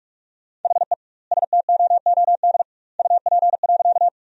Morse code, heard as a keyed tone, about 44 wpm, 700 Hz. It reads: HE STOOD UP.
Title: 2x Word-Spacing - Sentences from Top 500 Words - 45wpm